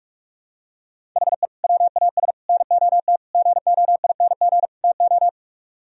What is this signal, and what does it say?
Morse code, heard as a keyed tone, about 49 wpm, 700 Hz. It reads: HE WAS NOT GOING TO